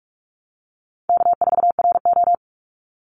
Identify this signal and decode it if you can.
Morse code, heard as a keyed tone, about 46 wpm, 700 Hz. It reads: K4RO
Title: Easy Call Signs - 45wpm